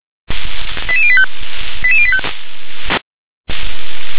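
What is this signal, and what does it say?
Five-tone selective call (ZVEI), heard as quick runs of five tones, tones of about 70 ms. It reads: 88085 88085
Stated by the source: Five-tone ZVEI selective call received on an SDR.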